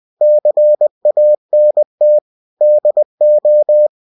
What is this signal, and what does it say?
Morse code, heard as a keyed tone, about 20 wpm, 600 Hz. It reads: CANT DO
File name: cw-books / winnie-the-pooh / winnie-20wpm-page-0016